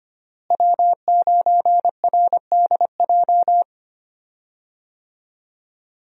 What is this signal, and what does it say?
Morse code, heard as a keyed tone, about 25 wpm, 700 Hz. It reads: W9RDJ
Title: USA Call Signs - 5 Characters - Repeated 2x - 25wpm